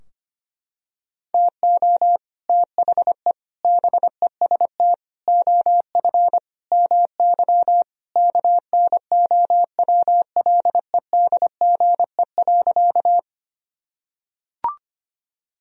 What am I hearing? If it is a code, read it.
Morse code, heard as a keyed tone, about 25 wpm, 700 Hz. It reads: TO THE BEST OF MY KNOWLEDGE.
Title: Sentences from Top 1000 Words - Review Part 3 - 25wpm